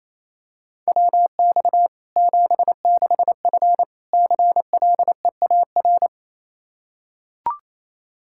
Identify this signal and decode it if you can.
Morse code, heard as a keyed tone, about 28 wpm, 700 Hz. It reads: WX 76F CLEAR
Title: K1USN SST QSO Practice 28wpm